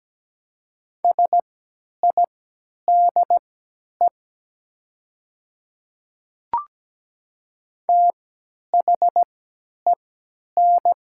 Morse code, heard as a keyed tone, about 17 wpm, spaced as for 10 wpm, 700 Hz. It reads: SIDE THEN
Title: 3x Character Spacing - Top 100 Words - 17wpm